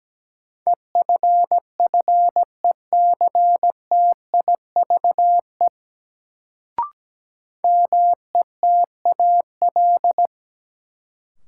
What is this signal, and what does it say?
Morse code, heard as a keyed tone, about 17 wpm, 700 Hz. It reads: EFFECTIVE METAL
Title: Sentences from Top 1000 Words - Review Part 6 - New Word Drill - 17wpm